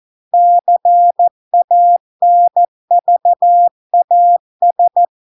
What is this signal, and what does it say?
Morse code, heard as a keyed tone, about 14 wpm, 700 Hz. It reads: CANVAS